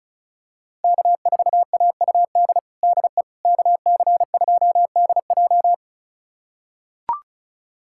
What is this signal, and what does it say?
Morse code, heard as a keyed tone, about 34 wpm, 700 Hz. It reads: K4AUD DE KC2DJ
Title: USA Callsign DE Callsign - 5 Characters - 35wpm